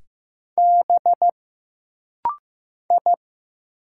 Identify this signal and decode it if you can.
Morse code, heard as a keyed tone, about 15 wpm, 700 Hz. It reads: B I
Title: Single Letter-Number Alt - Rapid-Fire 15wpm